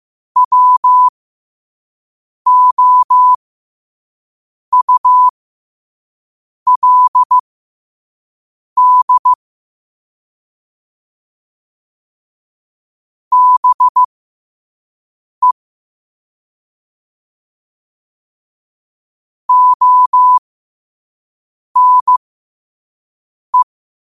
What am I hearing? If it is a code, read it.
Morse code, heard as a keyed tone, about 15 wpm, spaced as for 5 wpm, 1000 Hz. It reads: WOULD BE ONE
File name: qotdfiles / 5wpm